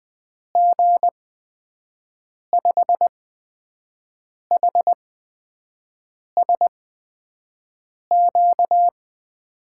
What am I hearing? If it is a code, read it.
Morse code, heard as a keyed tone, about 20 wpm, spaced as for 5 wpm, 700 Hz. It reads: G5HSQ
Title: USA Call Signs - 6 Characters - 8x Character Spacing - 20wpm